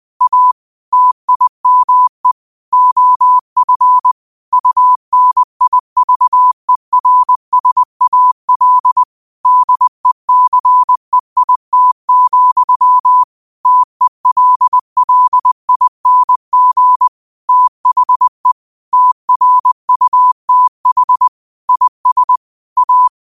Morse code, heard as a keyed tone, about 20 wpm, 1000 Hz. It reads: A TIME OF UNIVERSAL DECEIT, TELLING THE TRUTH IS A